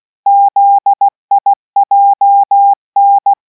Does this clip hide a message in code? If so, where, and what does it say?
Yes: Morse code from 0.3 to 3.3 s, heard as a keyed tone, about 16 wpm, 800 Hz. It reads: ZIJN